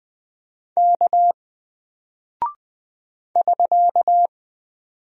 Morse code, heard as a keyed tone, about 20 wpm, 700 Hz. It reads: K <SK>
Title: Single Character (Letter:Number:Punctuation:Prosign) Alt 20wpm